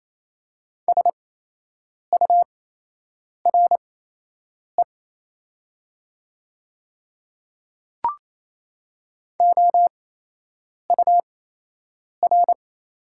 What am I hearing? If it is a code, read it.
Morse code, heard as a keyed tone, about 28 wpm, spaced as for 8 wpm, 700 Hz. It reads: SURE OUR
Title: 8x Character Spacing - Top 300 Words - 28wpm